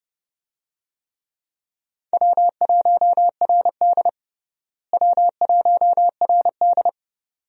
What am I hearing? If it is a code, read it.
Morse code, heard as a keyed tone, about 31 wpm, 700 Hz. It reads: W1RD W1RD
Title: USA Call Signs - 4 Characters - Repeated 2x - 30wpm